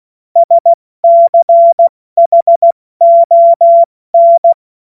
Morse code, heard as a keyed tone, about 16 wpm, 680 Hz. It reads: SCHON
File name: Morsekurs / Witze / Witz 17. August 2021, 70 F80